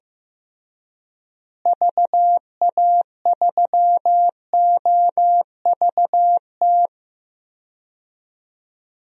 Morse code, heard as a keyed tone, about 15 wpm, 700 Hz. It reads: VA3OVT
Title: DXCC-100 Call Signs - 6 Characters - Repeated 3x - 15wpm